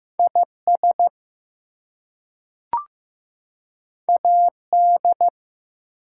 Morse code, heard as a keyed tone, about 15 wpm, 700 Hz. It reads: IS AD
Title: Two Letter Words - 15wpm